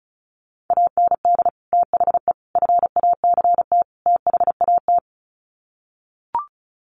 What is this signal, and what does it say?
Morse code, heard as a keyed tone, about 34 wpm, 700 Hz. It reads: AND THE FACT THAT
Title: Sets of 4 Words 35wpm